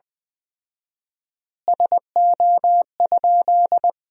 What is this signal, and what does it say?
Morse code, heard as a keyed tone, about 20 wpm, 700 Hz. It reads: SO?
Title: 4x Word-Spacing - Sentences from Top 200 Words - 20wpm